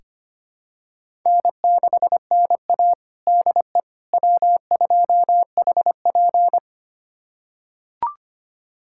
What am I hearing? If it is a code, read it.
Morse code, heard as a keyed tone, about 25 wpm, 700 Hz. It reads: N6NA DE W2HP
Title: USA Callsign DE Callsign - 4 Characters - 25wpm